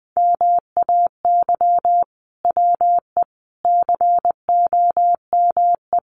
Morse code, heard as a keyed tone, about 19 wpm, 700 Hz. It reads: MAY WE COME
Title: Sentences from Top 100 Words 20wpm